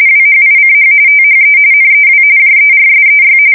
RTTY, about 45.45 baud. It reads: WHICH THE NEW JERSEY
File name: fsk170 / battleshipnj